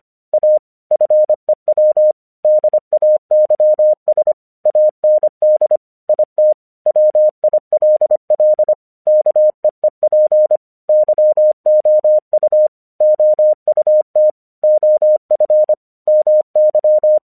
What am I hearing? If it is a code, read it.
Morse code, heard as a keyed tone, about 24 wpm, 610 Hz. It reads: A FEW DAYS AND IT WILL KEEP YOU OUT OF MY
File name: 1 English 25wpm / AQC21